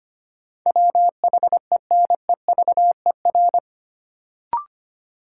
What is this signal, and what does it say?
Morse code, heard as a keyed tone, about 25 wpm, 700 Hz. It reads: WHENEVER
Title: Sentences from Top 1200 Words - New Word Drill - 25wpm